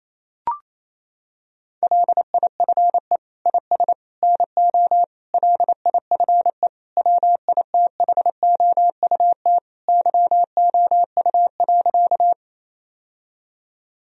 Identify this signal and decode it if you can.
Morse code, heard as a keyed tone, about 28 wpm, 700 Hz. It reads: LIFE IS NO LIFE WITHOUT YOU.
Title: Sentences from Top 1000 Words - Review Part 3 - 28wpm